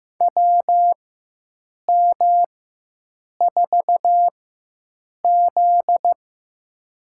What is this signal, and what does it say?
Morse code, heard as a keyed tone, about 15 wpm, spaced as for 7 wpm, 700 Hz. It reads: WM4Z
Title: USA Call Signs - 4 Characters - 4x Character Spacing - 15wpm